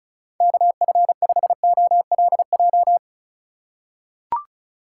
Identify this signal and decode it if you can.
Morse code, heard as a keyed tone, about 36 wpm, 700 Hz. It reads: KF5OLJ
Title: USA Call Signs - 6 Characters - 3x Character Spacing - 35wpm